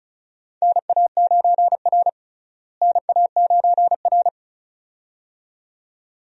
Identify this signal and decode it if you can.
Morse code, heard as a keyed tone, about 36 wpm, 700 Hz. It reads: NA9R NA9R
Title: USA Call Signs - 4 Characters - Repeated 2x - 35wpm